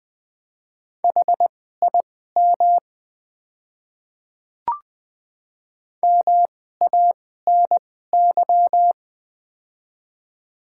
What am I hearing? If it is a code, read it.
Morse code, heard as a keyed tone, about 20 wpm, spaced as for 15 wpm, 700 Hz. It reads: HIM MANY
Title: 2x Character Spacing - Top 100 Words - 20wpm